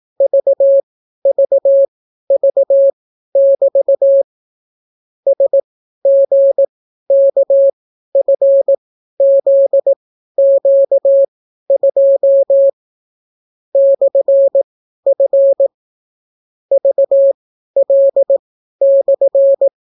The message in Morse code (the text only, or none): VVV<BT> SGKFZQ2 /F VL/